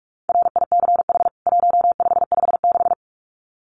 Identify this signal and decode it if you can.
Morse code, heard as a keyed tone, about 44 wpm, 700 Hz. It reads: RICH 1556
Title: CWT Exchange 45wpm